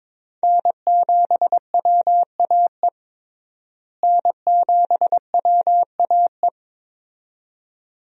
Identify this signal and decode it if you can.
Morse code, heard as a keyed tone, about 22 wpm, 700 Hz. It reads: N7WAE N7WAE
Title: USA Call Signs - 5 Characters - Repeated 2x - 22wpm